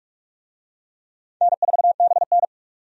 Morse code, heard as a keyed tone, about 46 wpm, 700 Hz. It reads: N4BN
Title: USA Call Signs - 4 Characters - 45wpm